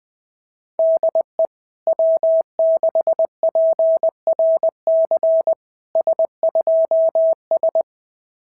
Morse code, heard as a keyed tone, about 20 wpm, 650 Hz. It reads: DE W6PRC S2S
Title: SOTA QSO Practice 20wpm